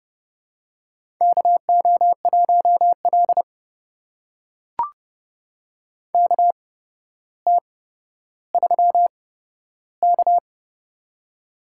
Morse code, heard as a keyed tone, about 30 wpm, spaced as for 13 wpm, 700 Hz. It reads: KO1L KT3K
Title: USA Call Signs - 4 Characters - 8x Character Spacing - 30wpm